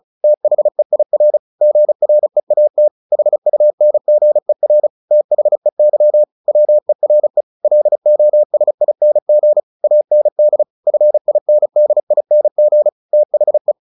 Morse code, heard as a keyed tone, about 34 wpm, 600 Hz. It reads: THEIR GREAT HUNGER THEY WERE LOSING AND FINDING THE
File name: the old man and the sea 2 35wpm